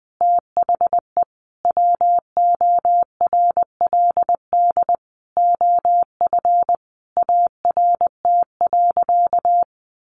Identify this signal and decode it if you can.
Morse code, heard as a keyed tone, about 19 wpm, 700 Hz. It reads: THE WORLD OF ART.
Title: Sentences from Top 1500 Words 20wpm